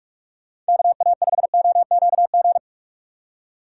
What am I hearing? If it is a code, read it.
Morse code, heard as a keyed tone, about 57 wpm, 700 Hz. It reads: KA5OQG